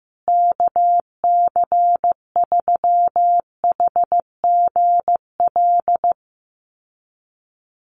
Morse code, heard as a keyed tone, about 15 wpm, 700 Hz. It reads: KC3HGL
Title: USA Call Signs Repeated 2x 15wpm